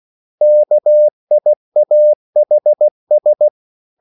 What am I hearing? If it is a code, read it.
Morse code, heard as a keyed tone, about 16 wpm, 600 Hz. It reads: KIAHS